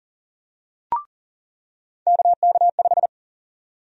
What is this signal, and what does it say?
Morse code, heard as a keyed tone, about 40 wpm, 700 Hz. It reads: KK5
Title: USA Call Sign Prefixes - 40wpm